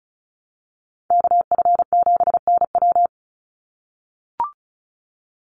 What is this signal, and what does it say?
Morse code, heard as a keyed tone, about 34 wpm, 700 Hz. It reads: KF7NW
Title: USA Call Signs 35wpm